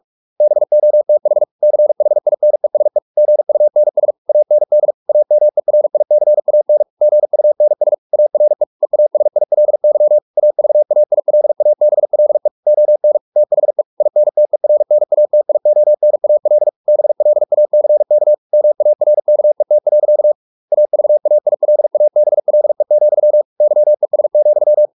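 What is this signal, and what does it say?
Morse code, heard as a keyed tone, about 44 wpm, 600 Hz. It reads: BOTH CHINESE GUNS AND AMERICAN GUNS ARE EASILY AVAILABLE ON THE INTERNATIONAL BLACK MARKET. AVAILABLE, YES,
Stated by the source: Morse practice at 45 wpm, the novel Never by Ken Follett in English, from CW med Gnister.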